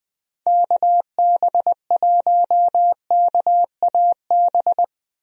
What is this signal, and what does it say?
Morse code, heard as a keyed tone, about 20 wpm, 700 Hz. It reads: KB1KAB
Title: USA Call Signs - 6 Characters - 20wpm